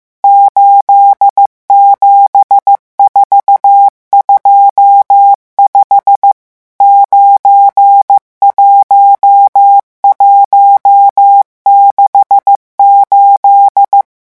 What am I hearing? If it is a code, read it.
Morse code, heard as a keyed tone, about 15 wpm, 800 Hz. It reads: 87425 91168
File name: morse mp3 / revision11 chiffres vitesse 14